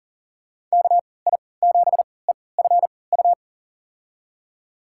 Morse code, heard as a keyed tone, about 40 wpm, spaced as for 23 wpm, 700 Hz. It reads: KI7EFU
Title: USA Call Signs - 3x Character Spacing - 40wpm